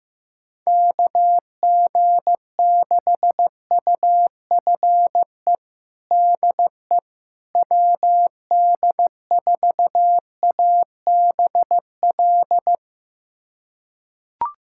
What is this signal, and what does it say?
Morse code, heard as a keyed tone, about 16 wpm, 700 Hz. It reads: KG6UFE DE WD4ABL
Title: USA Callsign DE Callsign 15wpm